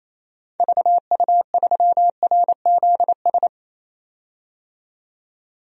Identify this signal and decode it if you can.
Morse code, heard as a keyed tone, about 28 wpm, 700 Hz. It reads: VU3RZS
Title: DXCC-100 Call Signs - 6 Characters - 28wpm